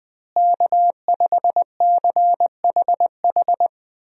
Morse code, heard as a keyed tone, about 20 wpm, 700 Hz. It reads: K5CHH
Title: USA Call Signs - 5 Characters - Repeated 3x - 20wpm